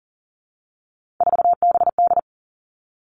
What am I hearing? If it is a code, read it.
Morse code, heard as a keyed tone, about 38 wpm, 690 Hz. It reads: 4BD
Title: Three Challenging Letter-Number Combinations 40wpm